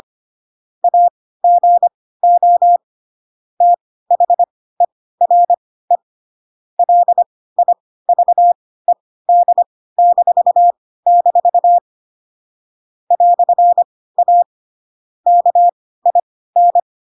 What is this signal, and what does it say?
Morse code, heard as a keyed tone, about 25 wpm, spaced as for 16 wpm, 700 Hz. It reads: AGO THERE LIVED-- "A KIN
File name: Pinocchio / Pinocchio ch1 10 25 ik1hge